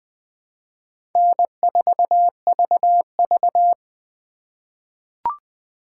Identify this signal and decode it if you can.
Morse code, heard as a keyed tone, about 20 wpm, 700 Hz. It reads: N4VV
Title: USA Call Signs - 2x Character Spacing - 20wpm